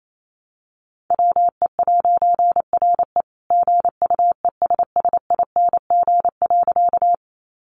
Morse code, heard as a keyed tone, about 28 wpm, 700 Hz. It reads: WE'RE GUESSING.
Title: A Sound Of Thunder 28wpm